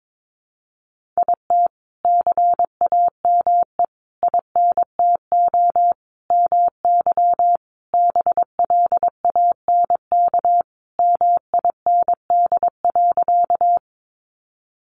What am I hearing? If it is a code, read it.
Morse code, heard as a keyed tone, about 22 wpm, 700 Hz. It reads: IT CAME INTO MY BLANK MIND.
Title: Sentences from Top 1500 Words 22wpm